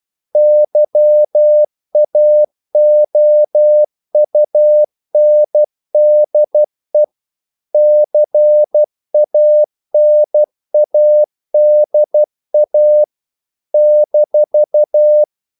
Morse code, heard as a keyed tone, about 12 wpm, 600 Hz. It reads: YAOUNDE CANADA -
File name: CW / Practice / NationalCapitals12